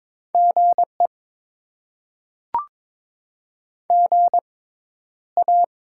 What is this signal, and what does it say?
Morse code, heard as a keyed tone, about 21 wpm, spaced as for 11 wpm, 700 Hz. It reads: GE GA
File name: Top 20 QSO Elements - 6x Character Spacing - 22wpm